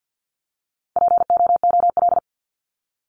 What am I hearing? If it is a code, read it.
Morse code, heard as a keyed tone, about 50 wpm, 700 Hz. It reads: POOL
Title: Top 1751-2000 Words 50wpm